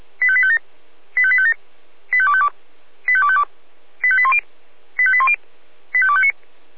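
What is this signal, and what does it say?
Five-tone selective call (ZVEI), heard as quick runs of five tones, tones of about 70 ms. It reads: 85857 85857 85252 85252 85810 85810 85259
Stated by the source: ZVEI five-tone call received by SDR.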